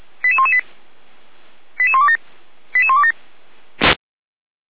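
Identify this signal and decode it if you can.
Five-tone selective call (ZVEI), heard as quick runs of five tones, tones of about 70 ms. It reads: 88118 88127 88127
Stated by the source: ZVEI five-tone selective call received on SDR.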